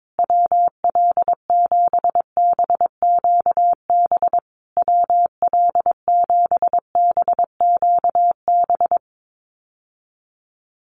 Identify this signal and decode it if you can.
Morse code, heard as a keyed tone, about 22 wpm, 700 Hz. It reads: WL7BQB WL7BQB
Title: USA Call Signs Repeated 3x 22wpm